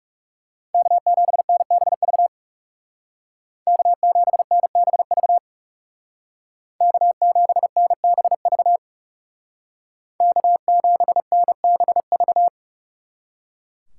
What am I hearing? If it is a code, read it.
Morse code, heard as a keyed tone, about 37 wpm, 700 Hz. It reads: K7NBV K7NBV K7NBV K7NBV
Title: Speed Racing - USA 1x3 Call Signs 30wpm